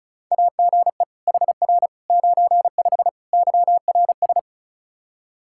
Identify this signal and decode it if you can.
Morse code, heard as a keyed tone, about 36 wpm, 700 Hz. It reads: AGE HR 95 YRS
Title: Practice QSO - Exchange 3 - More Info - 35wpm